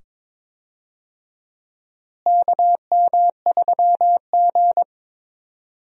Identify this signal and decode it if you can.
Morse code, heard as a keyed tone, about 22 wpm, 700 Hz. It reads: KM3G
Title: USA Call Signs - 4 Characters - Repeated 2x - 22wpm